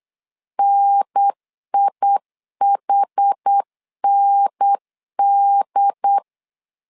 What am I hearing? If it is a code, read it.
Morse code, heard as a keyed tone, about 8 wpm, 800 Hz. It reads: NIHND
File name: LEZIONI / 5 LBVC / 5 L 8wpm 2831